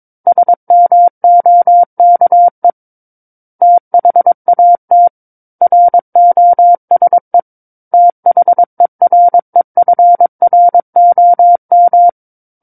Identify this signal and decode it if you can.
Morse code, heard as a keyed tone, about 22 wpm, 700 Hz. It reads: SMOKE THAT ROSE THEREFROM
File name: Book One / 13-Chapter 13 - 22 WPM